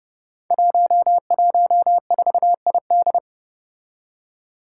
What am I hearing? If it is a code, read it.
Morse code, heard as a keyed tone, about 30 wpm, 700 Hz. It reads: 114ID
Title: Five Challenging Letter-Number Combinations 30wpm